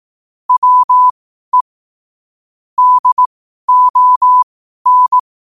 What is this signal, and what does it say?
Morse code, heard as a keyed tone, about 18 wpm, spaced as for 12 wpm, 1000 Hz. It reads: WE DON
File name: qotdfiles / 12wpm